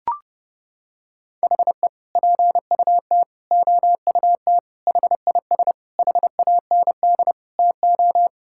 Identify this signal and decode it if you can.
Morse code, heard as a keyed tone, about 30 wpm, 700 Hz. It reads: HE PUT OUT HIS HAND TO
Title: Sentences from Top 1000 Words - Review Part 5 - 30wpm